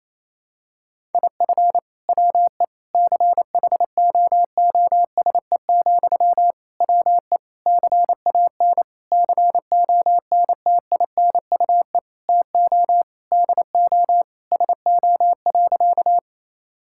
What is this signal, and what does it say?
Morse code, heard as a keyed tone, about 28 wpm, 700 Hz. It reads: IF WE CHOOSE, WE CAN CONTINUE TO DO SO.
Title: Sentences from Top 1000 Words - Review Part 6 - 28wpm